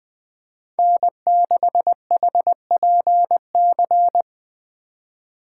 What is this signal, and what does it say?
Morse code, heard as a keyed tone, about 20 wpm, 700 Hz. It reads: N6HPC
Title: USA Call Signs - 4x Character Spacing - 20wpm